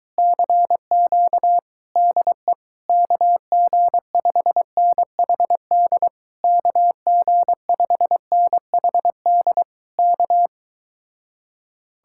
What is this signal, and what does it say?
Morse code, heard as a keyed tone, about 22 wpm, 700 Hz. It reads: CQ DE KG5NHD KG5NHD K